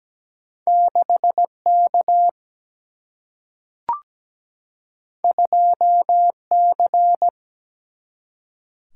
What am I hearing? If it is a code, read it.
Morse code, heard as a keyed tone, about 17 wpm, 700 Hz. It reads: 6K 2C
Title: Single-Letter Single-Number 17wpm